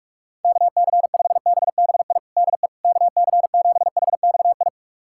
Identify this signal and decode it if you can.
Morse code, heard as a keyed tone, about 44 wpm, 700 Hz. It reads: KC5BBI DE KC7HXI
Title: USA Callsign DE Callsign - 6 Characters - 45wpm